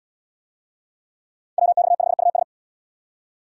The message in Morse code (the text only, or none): WX5US